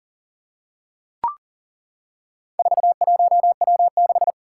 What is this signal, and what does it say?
Morse code, heard as a keyed tone, about 40 wpm, 700 Hz. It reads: 41W6
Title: Four Challenging Letter-Number Combinations 40wpm